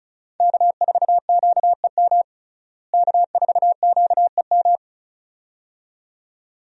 Morse code, heard as a keyed tone, about 36 wpm, 700 Hz. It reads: K4QEM K4QEM
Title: USA Call Signs - 5 Characters - Repeated 2x - 35wpm